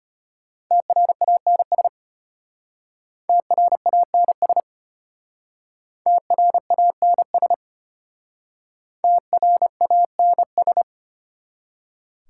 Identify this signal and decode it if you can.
Morse code, heard as a keyed tone, about 32 wpm, 700 Hz. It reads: TRANS TRANS TRANS TRANS